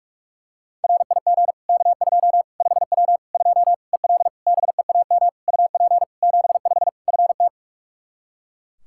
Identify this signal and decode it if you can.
Morse code, heard as a keyed tone, about 41 wpm, 700 Hz. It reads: RIG K2 5W 2 EL BEAM UP 75 FT
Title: Practice QSO - Exchange 2 - Info - 45wpm